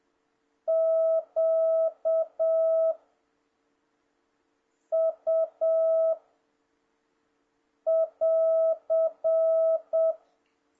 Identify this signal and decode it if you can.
Morse code, heard as a keyed tone, about 7 wpm, 650 Hz. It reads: Q U <AR>
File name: jh8cbh / 2026cw